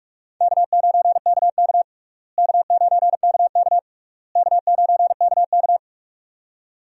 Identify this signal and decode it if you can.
Morse code, heard as a keyed tone, about 42 wpm, 700 Hz. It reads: K9KK K9KK K9KK
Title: USA Call Signs - 4 Characters - Repeated 3x - 45wpm